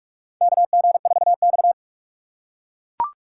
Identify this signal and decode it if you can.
Morse code, heard as a keyed tone, about 44 wpm, 700 Hz. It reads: KG4X
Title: USA Call Signs - 4 Characters - 6x Character Spacing - 45wpm